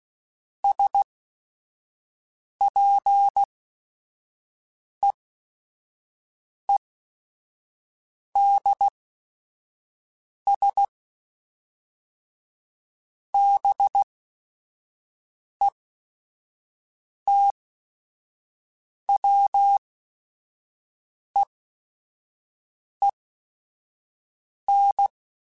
Morse code, heard as a keyed tone, about 16 wpm, spaced as for 5 wpm, 800 Hz. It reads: SPEEDS BETWEEN
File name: Weather Sets / Weather Sets-mp3s / Weather 4 10 @5